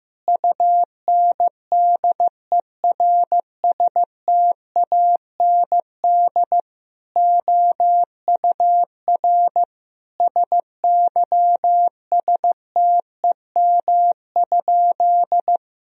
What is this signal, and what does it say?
Morse code, heard as a keyed tone, about 16 wpm, 700 Hz. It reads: UNDERSTAND OUR SYSTEM?
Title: Sentences from Top 1000 Words - Review Part 3 - 15wpm